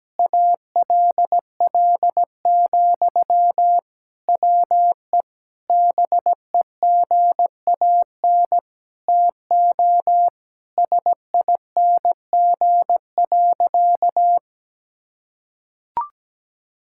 Morse code, heard as a keyed tone, about 17 wpm, 700 Hz. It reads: ALL, WE BEGAN TO SING.
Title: Sentences from Top 1600 Words 17wpm